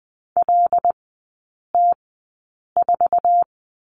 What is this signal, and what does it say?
Morse code, heard as a keyed tone, about 19 wpm, spaced as for 8 wpm, 700 Hz. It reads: LT4
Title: 100 Lesson 17e - Letter u - Three Characters 20wpm